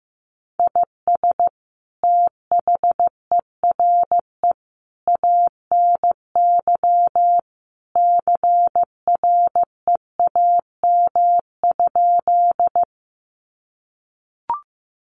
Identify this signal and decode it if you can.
Morse code, heard as a keyed tone, about 15 wpm, 700 Hz. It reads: IS THERE ANY CREAM?
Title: Sentences from Top 1500 Words 15wpm